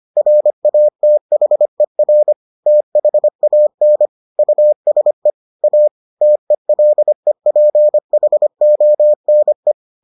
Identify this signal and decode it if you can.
Morse code, heard as a keyed tone, about 25 wpm, 600 Hz. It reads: RATHER THAN USE A TELEPHONE